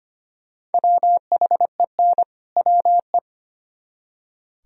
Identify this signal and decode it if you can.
Morse code, heard as a keyed tone, about 32 wpm, 700 Hz. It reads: WHEN WE